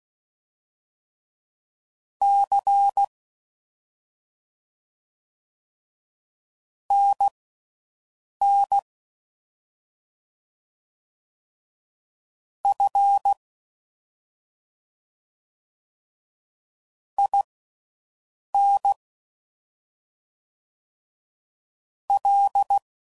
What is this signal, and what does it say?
Morse code, heard as a keyed tone, about 16 wpm, spaced as for 6 wpm, 800 Hz. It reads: C NN F IN L
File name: Mixed / CW Comparisons / Comparisons Broken Characters